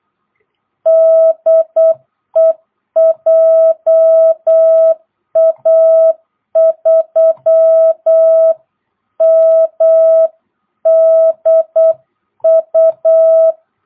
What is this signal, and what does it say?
Morse code, heard as a keyed tone, about 8 wpm, 660 Hz. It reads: DE JA3MDU
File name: jh8cbh / 2022cw